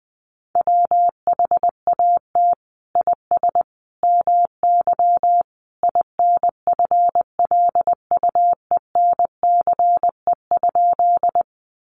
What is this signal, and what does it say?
Morse code, heard as a keyed tone, about 19 wpm, 700 Hz. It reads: WHAT IS MY INFLUENCE?
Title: Sentences from Top 1100 Words 20wpm